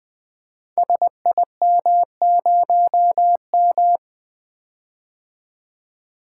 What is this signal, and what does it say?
Morse code, heard as a keyed tone, about 21 wpm, 700 Hz. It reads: SIM0M
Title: Code Groups - 5 Letter:Numbers - 20wpm